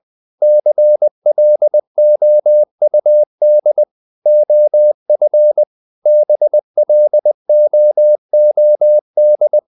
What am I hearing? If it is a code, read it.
Morse code, heard as a keyed tone, about 20 wpm, 600 Hz. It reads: CLOUD OF BLOOD
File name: the old man and the sea part 2 20wpm